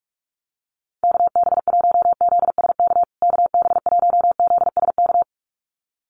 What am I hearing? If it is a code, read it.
Morse code, heard as a keyed tone, about 44 wpm, 690 Hz. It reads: KB1ZSK KB1ZSK
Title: USA Call Signs Repeated 2x 45wpm